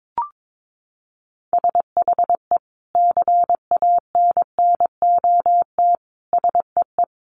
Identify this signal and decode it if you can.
Morse code, heard as a keyed tone, about 22 wpm, 700 Hz. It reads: SHE CANNOT SEE
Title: Additional Easy Phrases 22wpm